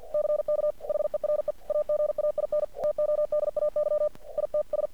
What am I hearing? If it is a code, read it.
Morse code, heard as a keyed tone, about 50 wpm, 600 Hz. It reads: CK HERE AGAIN TODAY ITS
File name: archived / CRSnet